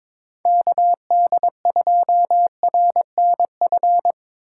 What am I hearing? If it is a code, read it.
Morse code, heard as a keyed tone, about 22 wpm, 700 Hz. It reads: KD2RNF
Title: USA Call Signs - 4x Character Spacing - 22wpm